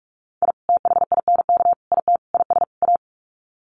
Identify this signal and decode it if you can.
Morse code, heard as a keyed tone, about 44 wpm, 700 Hz. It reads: I THINK IT IS A
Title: Sets of 5 Words Encore 45wpm